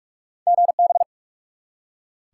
Morse code, heard as a keyed tone, about 46 wpm, 700 Hz. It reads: GB
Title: Top 90 QSO Elements 45wpm